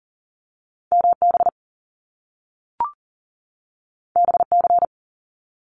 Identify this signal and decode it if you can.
Morse code, heard as a keyed tone, about 40 wpm, 700 Hz. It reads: MB BC
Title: Canadian Provinces and Territories 40wpm